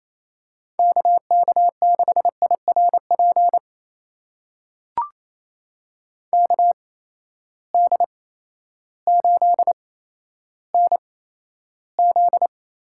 Morse code, heard as a keyed tone, about 28 wpm, spaced as for 13 wpm, 700 Hz. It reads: KK6IRP KD8NZ
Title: USA Call Signs - 6 Characters - 8x Character Spacing - 28wpm